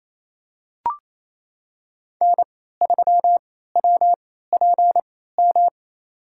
Morse code, heard as a keyed tone, about 28 wpm, spaced as for 16 wpm, 700 Hz. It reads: N3WPM
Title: USA Call Signs - 3x Character Spacing - 28wpm